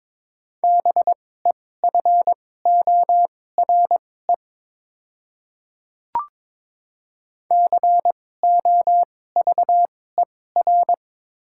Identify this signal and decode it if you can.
Morse code, heard as a keyed tone, about 22 wpm, spaced as for 16 wpm, 700 Hz. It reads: BEFORE COVER
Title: 2x Character Spacing - Top 300 Words - 22wpm